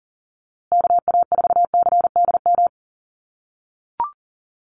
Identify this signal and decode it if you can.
Morse code, heard as a keyed tone, about 40 wpm, 700 Hz. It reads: KA4CDM
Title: USA Call Signs Repeated 2x 40wpm